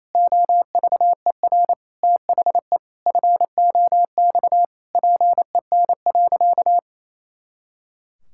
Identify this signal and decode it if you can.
Morse code, heard as a keyed tone, about 28 wpm, 700 Hz. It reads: OVER THE FOX PEN.
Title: Pangram Challenge v2 - 28wpm